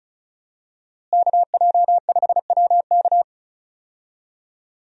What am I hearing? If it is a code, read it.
Morse code, heard as a keyed tone, about 36 wpm, 700 Hz. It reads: KJ5WK
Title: USA Call Signs - 5 Characters - Repeated 2x - 35wpm